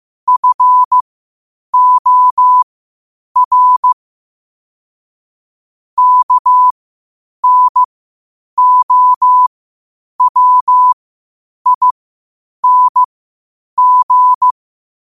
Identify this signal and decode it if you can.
Morse code, heard as a keyed tone, about 15 wpm, spaced as for 8 wpm, 1000 Hz. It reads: FOR KNOWING
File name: qotdfiles / 8wpm